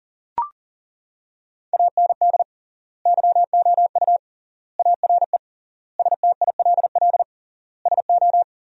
Morse code, heard as a keyed tone, about 40 wpm, 700 Hz. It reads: AND YOU ARE STILL SO
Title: 3x Word-Spacing - Sentences from Top 400 Words - 40wpm